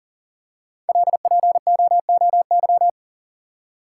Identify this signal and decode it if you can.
Morse code, heard as a keyed tone, about 40 wpm, 700 Hz. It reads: LPOOY
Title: Code Groups - 5 Letters - 40wpm